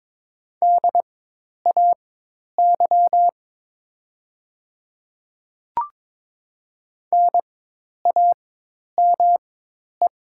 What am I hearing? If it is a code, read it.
Morse code, heard as a keyed tone, about 22 wpm, spaced as for 10 wpm, 700 Hz. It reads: DAY NAME
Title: 4x Character Spacing - Top 500 Words - 22wpm